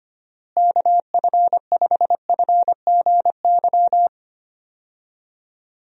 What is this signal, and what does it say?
Morse code, heard as a keyed tone, about 25 wpm, 700 Hz. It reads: KF5FGY
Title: USA Call Signs - 6 Characters - Repeated 2x - 25wpm